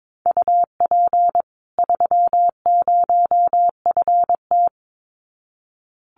Morse code, heard as a keyed tone, about 22 wpm, 700 Hz. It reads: UP 30FT
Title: Common QSO Phrases 22wpm